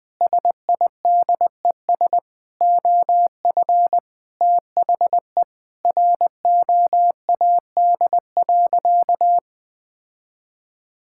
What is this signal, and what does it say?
Morse code, heard as a keyed tone, about 21 wpm, 700 Hz. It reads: SIDES OF THE ROAD.
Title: Sentences with 5-Word Phrases - 20wpm